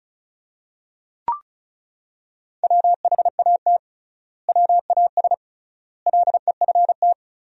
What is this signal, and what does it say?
Morse code, heard as a keyed tone, about 36 wpm, 700 Hz. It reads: WHAT WAS LEFT
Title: 3x Word-Spacing - Sentences from Top 300 Words - 35wpm